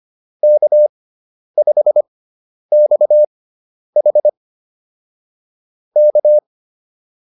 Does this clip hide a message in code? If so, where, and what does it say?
Yes: Morse code from 0.4 to 6.4 s, heard as a keyed tone, about 25 wpm, spaced as for 10 wpm, 600 Hz. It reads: K5XH K